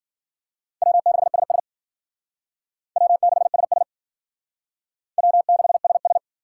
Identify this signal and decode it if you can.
Morse code, heard as a keyed tone, about 50 wpm, 700 Hz. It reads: W6SS W6SS W6SS
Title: Speed Racing - USA 1x2 Call Signs 40wpm